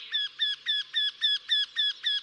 There is no code.